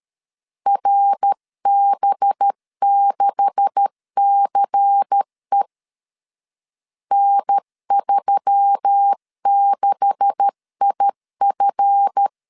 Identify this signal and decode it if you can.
Morse code, heard as a keyed tone, about 13 wpm, 800 Hz. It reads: RB6CE N36IF
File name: LEZIONI / 9 286 12wpm 2305